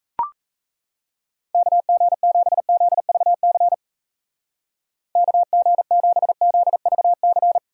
Morse code, heard as a keyed tone, about 36 wpm, 700 Hz. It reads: KG7ZVC KG7ZVC